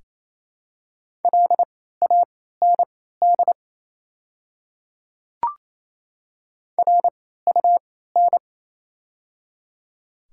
Morse code, heard as a keyed tone, about 28 wpm, spaced as for 16 wpm, 700 Hz. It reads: LAND RUN
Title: 3x Character Spacing - Top 500 Words - 28wpm